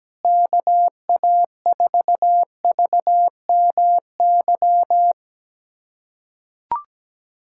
Morse code, heard as a keyed tone, about 17 wpm, 700 Hz. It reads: KA4VMY
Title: USA Callsign DE Callsign 17wpm